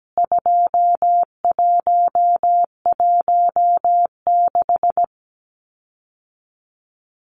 Morse code, heard as a keyed tone, about 17 wpm, 700 Hz. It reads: 2116
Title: CWT Exchange 17wpm